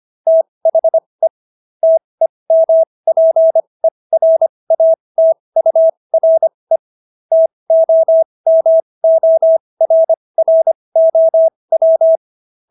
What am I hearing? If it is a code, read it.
Morse code, heard as a keyed tone, about 25 wpm, spaced as for 20 wpm, 650 Hz. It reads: THE TEMPERATURE TOMORROW